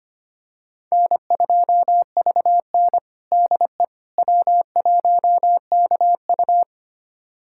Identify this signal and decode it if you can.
Morse code, heard as a keyed tone, about 25 wpm, 700 Hz. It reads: N2VN DE W1KU
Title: USA Callsign DE Callsign 25wpm